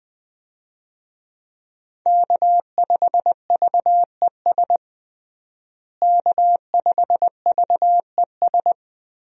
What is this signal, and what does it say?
Morse code, heard as a keyed tone, about 20 wpm, 700 Hz. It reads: K5VES K5VES
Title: USA Call Signs - 5 Characters - Repeated 3x - 20wpm